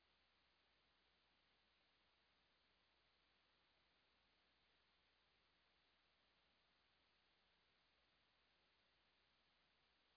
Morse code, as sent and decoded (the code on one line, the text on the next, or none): none